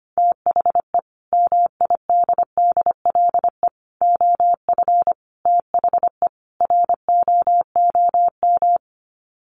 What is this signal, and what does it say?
Morse code, heard as a keyed tone, about 25 wpm, 700 Hz. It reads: THE MIDDLE OF THE ROOM
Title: Sets of 5 Words Encore2 25wpm